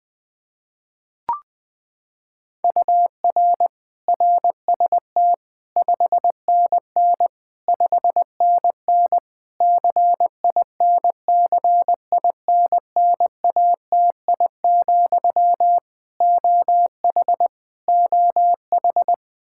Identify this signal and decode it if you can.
Morse code, heard as a keyed tone, about 21 wpm, 700 Hz. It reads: UR RST 5NN 5NN CINCINNATI, OH OH
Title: Practice QSO - Exchange 1 - Beginning - 20wpm